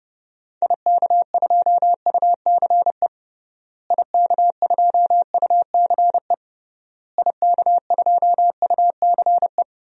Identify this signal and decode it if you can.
Morse code, heard as a keyed tone, about 30 wpm, 700 Hz. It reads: IK2UCE IK2UCE IK2UCE
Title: DXCC-100 Call Signs - 6 Characters - Repeated 3x - 30wpm